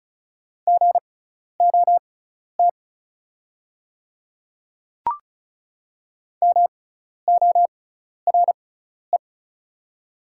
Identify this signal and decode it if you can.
Morse code, heard as a keyed tone, about 36 wpm, spaced as for 12 wpm, 690 Hz. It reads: GOT MORE
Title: 6x Character Spacing - Top 500 Words - 35wpm